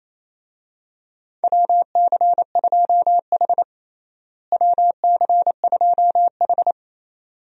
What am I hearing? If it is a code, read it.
Morse code, heard as a keyed tone, about 28 wpm, 700 Hz. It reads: WC2H WC2H
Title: USA Call Signs - 4 Characters - Repeated 2x - 28wpm